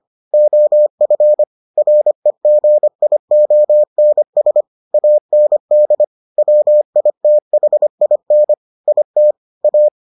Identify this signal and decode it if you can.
Morse code, heard as a keyed tone, about 25 wpm, 600 Hz. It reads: OF REGIONS AND WITHIN IT A